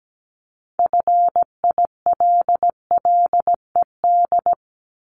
Morse code, heard as a keyed tone, about 17 wpm, 700 Hz. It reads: FILLED
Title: Multi-letter Phonemes 17wpm